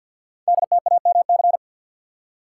Morse code, heard as a keyed tone, about 50 wpm, 700 Hz. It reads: DTAMC
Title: Code Groups - 5 Letters - 50wpm